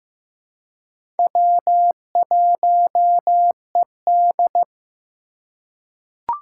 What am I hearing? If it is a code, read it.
Morse code, heard as a keyed tone, about 15 wpm, 700 Hz. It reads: W1ED
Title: USA Call Signs - 4 Characters - 3x Character Spacing - 15wpm